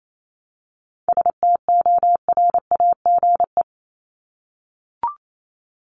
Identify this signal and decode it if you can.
Morse code, heard as a keyed tone, about 28 wpm, 700 Hz. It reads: STORAGE
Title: Top 1251-1500 Words 28wpm